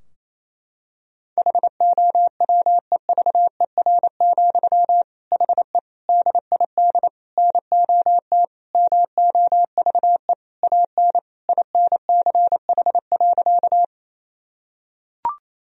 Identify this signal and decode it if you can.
Morse code, heard as a keyed tone, about 28 wpm, 700 Hz. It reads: HOWEVER, HE DID NOT MOVE AN INCH.
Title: Sentences from Top 1000 Words - Review Part 3 - 28wpm